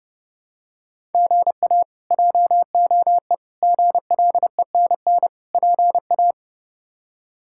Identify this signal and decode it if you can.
Morse code, heard as a keyed tone, about 31 wpm, 700 Hz. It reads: GA JOE GLENN PA
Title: K1USN SST QSO Practice - 30wpm